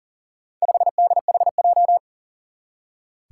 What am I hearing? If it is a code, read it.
Morse code, heard as a keyed tone, about 40 wpm, 700 Hz. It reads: 5DHJ
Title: Four Challenging Letter-Number Combinations 40wpm